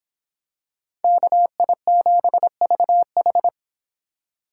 Morse code, heard as a keyed tone, about 24 wpm, 700 Hz. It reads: KI7VH